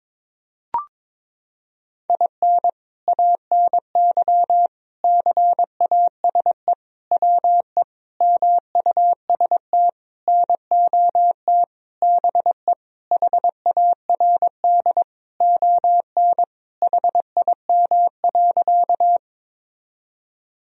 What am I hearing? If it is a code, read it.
Morse code, heard as a keyed tone, about 22 wpm, 700 Hz. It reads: IN ANY CASE WE MUST NOT BE HARD ON HIM.
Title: Sentences from Top 1000 Words - Review Part 1 - 22wpm